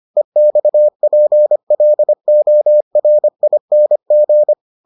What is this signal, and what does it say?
Morse code, heard as a keyed tone, about 25 wpm, 600 Hz. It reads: EXPLORING